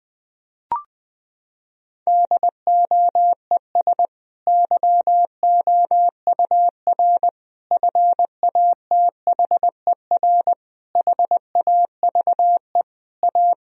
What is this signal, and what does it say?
Morse code, heard as a keyed tone, about 20 wpm, 700 Hz. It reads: DOES YOUR FATHER HAVE A
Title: Sentences from Top 1000 Words - Review Part 3 - 20wpm